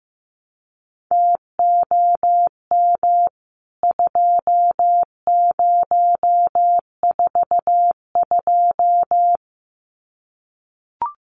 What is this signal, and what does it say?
Morse code, heard as a keyed tone, about 14 wpm, 700 Hz. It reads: TOM 2042
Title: CWT Exchange 15wpm